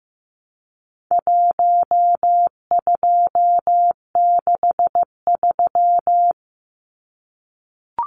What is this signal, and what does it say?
Morse code, heard as a keyed tone, about 15 wpm, 700 Hz. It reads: 1263
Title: Numbers - 4-Digit 15wpm